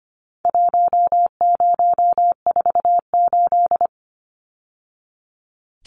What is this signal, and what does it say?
Morse code, heard as a keyed tone, about 25 wpm, 700 Hz. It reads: 1048
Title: Numbers - Contest Style - 4-Digit 25wpm